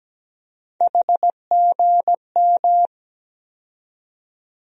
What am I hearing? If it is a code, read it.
Morse code, heard as a keyed tone, about 17 wpm, 700 Hz. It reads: HGM
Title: USA Call Sign Postfixes - 3 Letters 17wpm